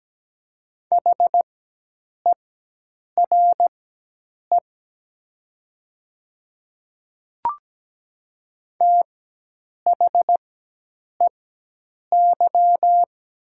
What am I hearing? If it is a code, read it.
Morse code, heard as a keyed tone, about 17 wpm, spaced as for 8 wpm, 700 Hz. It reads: HERE THEY
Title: 4x Character Spacing - Top 200 Words - 17wpm